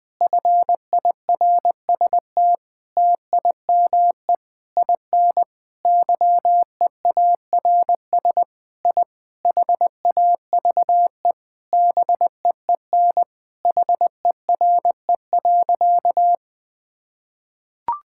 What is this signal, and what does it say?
Morse code, heard as a keyed tone, about 21 wpm, 700 Hz. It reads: FIRST TIME IN YEARS I HAVE BEEN HERE.
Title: Sentences with 4-Word Phrases - 20wpm